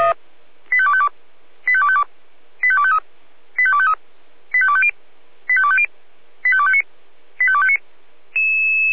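Five-tone selective call (ZVEI), heard as quick runs of five tones, tones of about 70 ms. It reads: 85252 85252 85253 85253 85250 85250 85259 85259
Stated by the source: Received by SDR, ZVEI five-tone selective call 85216.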